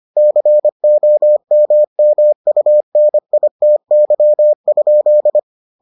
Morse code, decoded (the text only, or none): COMMUNITY?